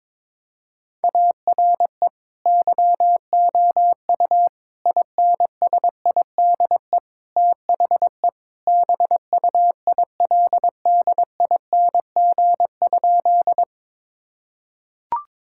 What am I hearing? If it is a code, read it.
Morse code, heard as a keyed tone, about 21 wpm, 700 Hz. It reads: ARE YOU INSIDE THE BUILDING?
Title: Sentences from Top 1000 Words - Review Part 6 - 22wpm